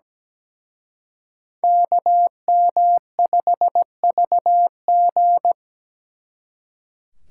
Morse code, heard as a keyed tone, about 20 wpm, 700 Hz. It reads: KM5VG
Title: Speed Racing - USA 2x2 Call Signs 17wpm